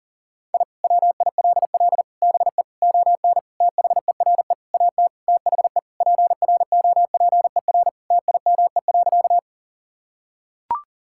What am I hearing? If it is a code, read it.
Morse code, heard as a keyed tone, about 40 wpm, 700 Hz. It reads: I WILL BE ON THERE AT THE PROPER TIME.
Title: Sentences from Top 1000 Words - Review Part 5 - 40wpm